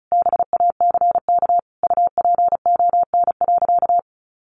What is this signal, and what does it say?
Morse code, heard as a keyed tone, about 34 wpm, 700 Hz. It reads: BACK UPON.
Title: Sentences from Top 1100 Words 35wpm